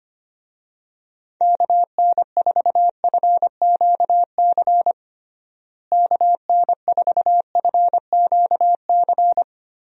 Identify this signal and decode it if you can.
Morse code, heard as a keyed tone, about 25 wpm, 700 Hz. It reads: KN4FQC KN4FQC
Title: USA Call Signs - 6 Characters - Repeated 2x - 25wpm